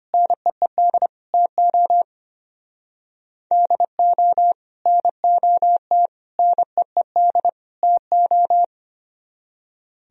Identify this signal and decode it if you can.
Morse code, heard as a keyed tone, about 31 wpm, 700 Hz. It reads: NEED TO DO NOT NEED TO
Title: Speed-Racing - Sets of 4 Words 25wpm